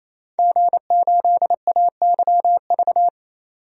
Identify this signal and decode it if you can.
Morse code, heard as a keyed tone, about 28 wpm, 700 Hz. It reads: G8AYV
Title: DXCC-100 Call Signs - 5 Characters - 28wpm